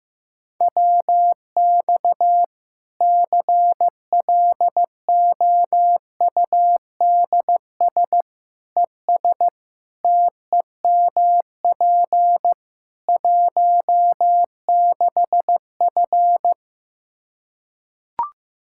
Morse code, heard as a keyed tone, about 16 wpm, 700 Hz. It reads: WX CLOUDS ES TEMP 16F
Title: Practice QSO - Exchange 2 - Info - 15wpm